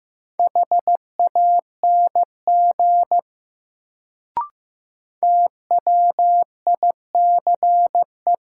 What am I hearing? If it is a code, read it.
Morse code, heard as a keyed tone, about 15 wpm, 700 Hz. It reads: HANG TWICE
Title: Sentences from Top 1600 Words - New Word Drill Alt - 15wpm